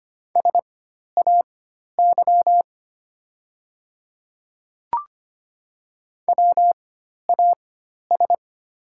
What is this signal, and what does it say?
Morse code, heard as a keyed tone, about 25 wpm, spaced as for 12 wpm, 700 Hz. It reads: SAY WAS
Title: 4x Character Spacing - Top 200 Words - 25wpm